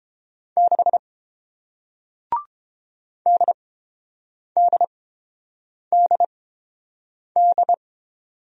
Morse code, heard as a keyed tone, about 30 wpm, spaced as for 7 wpm, 700 Hz. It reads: 6 DDDD